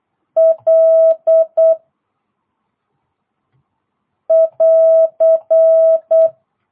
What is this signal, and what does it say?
Morse code, heard as a keyed tone, about 8 wpm, 650 Hz. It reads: L <AR>